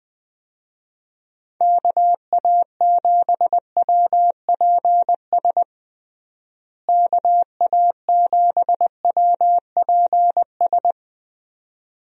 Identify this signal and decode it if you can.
Morse code, heard as a keyed tone, about 20 wpm, 700 Hz. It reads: KA7WPS KA7WPS
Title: USA Call Signs - 6 Characters - Repeated 2x - 20wpm